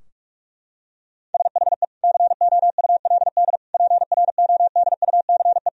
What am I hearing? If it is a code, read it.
Morse code, heard as a keyed tone, about 41 wpm, 700 Hz. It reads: SHE COULD PRODUCE
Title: Sentences from Top 1000 Words - Review Part 3 - 45wpm